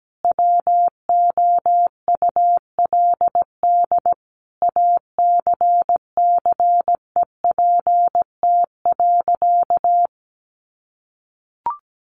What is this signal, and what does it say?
Morse code, heard as a keyed tone, about 17 wpm, 700 Hz. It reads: WOULD ACCEPT.
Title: Sentences from Top 1300 Words 17wpm